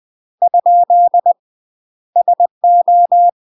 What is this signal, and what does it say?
Morse code, heard as a keyed tone, about 20 wpm, 700 Hz. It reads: ? SO
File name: UK MASTER QSOs TRAINING / 20WPM / UK MASTER QSOs 0039